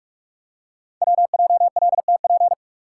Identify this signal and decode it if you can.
Morse code, heard as a keyed tone, about 44 wpm, 700 Hz. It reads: WJLTP
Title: Code Groups - 5 Letters - 45wpm